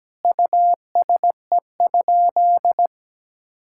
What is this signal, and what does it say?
Morse code, heard as a keyed tone, about 17 wpm, 700 Hz. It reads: USE?
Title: Sentences from Top 1000 Words - Review Part 5 - 17wpm